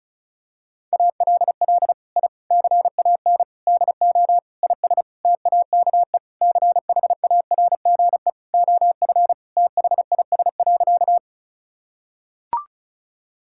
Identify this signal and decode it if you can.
Morse code, heard as a keyed tone, about 36 wpm, 700 Hz. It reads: ALL I CAN DO IS TAKE CHARGE OF THIS.
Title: Sentences from Top 1000 Words - Review Part 6 - 35wpm